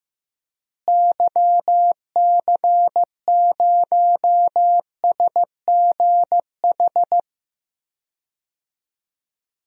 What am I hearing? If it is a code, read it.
Morse code, heard as a keyed tone, about 15 wpm, 700 Hz. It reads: YC0SGH
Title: DXCC-100 Call Signs - 6 Characters - Repeated 3x - 15wpm